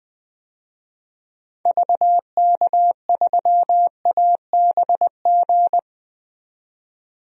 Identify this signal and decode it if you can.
Morse code, heard as a keyed tone, about 20 wpm, 700 Hz. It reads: VK3ABG
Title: DXCC-100 Call Signs - 6 Characters - 20wpm